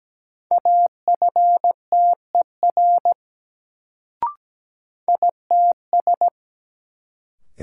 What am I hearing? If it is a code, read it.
Morse code, heard as a keyed tone, about 17 wpm, 700 Hz. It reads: AFTER ITS
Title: Sets of 2 Words - New Word Drill - 17wpm